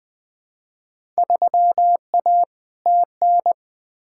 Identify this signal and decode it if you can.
Morse code, heard as a keyed tone, about 20 wpm, 700 Hz. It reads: 3A TN
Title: Field Day Exchanges - 20wpm